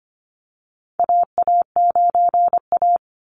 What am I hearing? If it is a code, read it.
Morse code, heard as a keyed tone, about 24 wpm, 700 Hz. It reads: AA9A
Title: Call Signs Repeated 2x - 25wpm